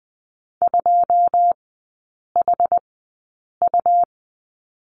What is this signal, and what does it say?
Morse code, heard as a keyed tone, about 20 wpm, spaced as for 8 wpm, 700 Hz. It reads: 2HU
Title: 138 Lesson 23e - Letter f - Three Characters 20wpm